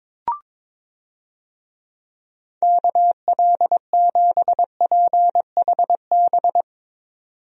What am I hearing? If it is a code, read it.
Morse code, heard as a keyed tone, about 22 wpm, 700 Hz. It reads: KL7PHB
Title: DXCC-100 Call Signs - 6 Characters - 22wpm